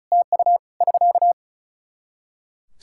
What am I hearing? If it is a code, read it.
Morse code, heard as a keyed tone, about 36 wpm, 700 Hz. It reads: TU <SK>
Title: Practice QSO - Exchange 4 - Ending - 35wpm